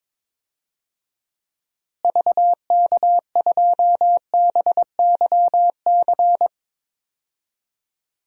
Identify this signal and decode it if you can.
Morse code, heard as a keyed tone, about 21 wpm, 700 Hz. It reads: VK2BYC
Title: DXCC-100 Call Signs - 6 Characters - Repeated 3x - 22wpm